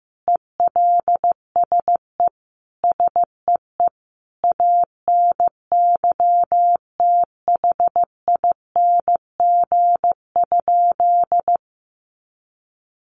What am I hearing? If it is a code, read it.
Morse code, heard as a keyed tone, about 15 wpm, 700 Hz. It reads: ELSE SEE ANYTHING?
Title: Sentences from Top 1000 Words 15wpm